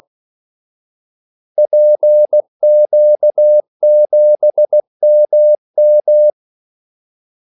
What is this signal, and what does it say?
Morse code, heard as a keyed tone, about 16 wpm, 600 Hz. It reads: PQ7MM
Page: Kaldesignaler 16 wpm